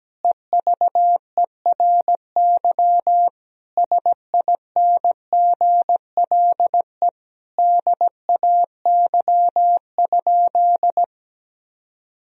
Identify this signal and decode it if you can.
Morse code, heard as a keyed tone, about 17 wpm, 700 Hz. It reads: EVERY SINGLE DAY?
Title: Sentences from Top 1000 Words - Review Part 4 - 17wpm